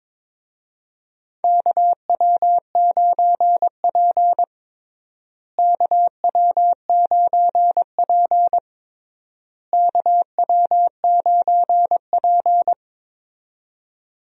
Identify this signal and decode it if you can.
Morse code, heard as a keyed tone, about 22 wpm, 700 Hz. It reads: KW9P KW9P KW9P
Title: USA Call Signs - 4 Characters - Repeated 3x - 22wpm